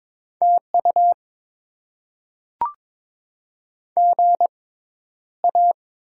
Morse code, heard as a keyed tone, about 22 wpm, spaced as for 11 wpm, 700 Hz. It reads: TU GA
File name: Top 20 QSO Elements - 6x Character Spacing - 22wpm